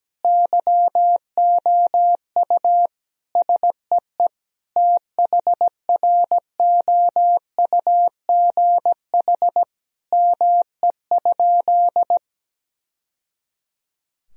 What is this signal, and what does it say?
Morse code, heard as a keyed tone, about 17 wpm, 700 Hz. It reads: YOU SEE THROUGH ME?
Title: Sentences from Top 1000 Words - Review Part 1 - 17wpm